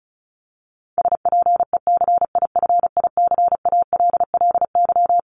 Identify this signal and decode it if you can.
Morse code, heard as a keyed tone, about 34 wpm, 700 Hz. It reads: SPECIFICALLY
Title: Top 1001-1250 Words 35wpm